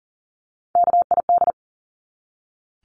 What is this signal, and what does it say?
Morse code, heard as a keyed tone, about 40 wpm, 700 Hz. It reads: KID
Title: Top 1501-1750 Words 40wpm